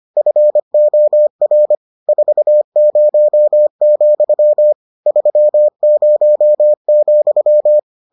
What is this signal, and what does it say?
Morse code, heard as a keyed tone, about 25 wpm, 600 Hz. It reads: FOR 40, 30,